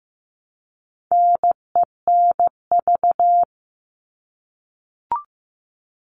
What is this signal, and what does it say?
Morse code, heard as a keyed tone, about 15 wpm, 700 Hz. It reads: NENV
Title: All Cut Numbers - 4-Digit 15wpm